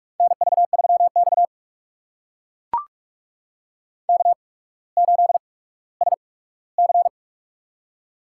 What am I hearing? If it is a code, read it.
Morse code, heard as a keyed tone, about 44 wpm, spaced as for 19 wpm, 700 Hz. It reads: NV3X K8SC
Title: USA Call Signs - 4 Characters - 8x Character Spacing - 45wpm